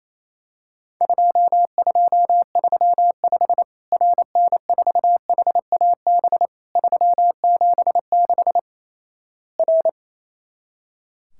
Morse code, heard as a keyed tone, about 28 wpm, 700 Hz. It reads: 2235 RN4HAB 376 R
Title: Worked All Europe DX Contest - QTC Exchange - 28wpm